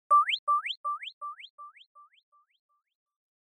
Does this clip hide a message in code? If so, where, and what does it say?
No.